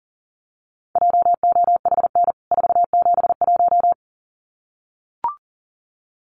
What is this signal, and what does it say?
Morse code, heard as a keyed tone, about 40 wpm, 700 Hz. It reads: JOHN 471
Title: CWT Exchange 40wpm